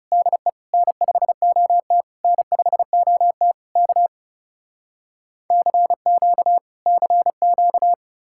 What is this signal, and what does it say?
Morse code, heard as a keyed tone, about 37 wpm, 700 Hz. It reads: DE N5OT N5OT K CQ CQ
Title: Speed-Racing Common QSO Phrases - 30wpm